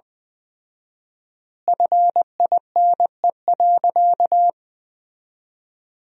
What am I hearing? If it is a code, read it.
Morse code, heard as a keyed tone, about 20 wpm, 700 Hz. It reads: FINE.
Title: 4x Word-Spacing - Sentences from Top 500 Words - 20wpm